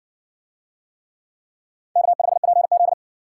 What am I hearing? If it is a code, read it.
Morse code, heard as a keyed tone, about 56 wpm, 700 Hz. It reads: K5PZ